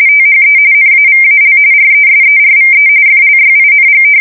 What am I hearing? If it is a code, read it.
RTTY, about 45.45 baud. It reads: THE NAVAL RESERVE FLYING
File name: fsk170 / usnavy4